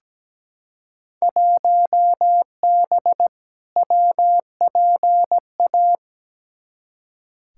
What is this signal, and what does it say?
Morse code, heard as a keyed tone, about 17 wpm, 700 Hz. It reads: 1B WPA
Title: Field Day Exchanges Encore - 17wpm